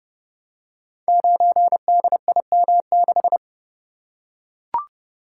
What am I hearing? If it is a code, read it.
Morse code, heard as a keyed tone, about 30 wpm, 700 Hz. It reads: 9DIM6
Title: Code Groups - 5 Letter:Numbers - 30wpm